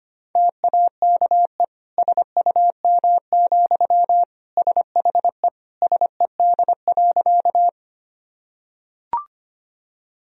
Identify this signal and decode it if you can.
Morse code, heard as a keyed tone, about 25 wpm, 700 Hz. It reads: TAKE SUM, SHE SED.
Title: Sentences from Top 200 Words and Abbreviations 25wpm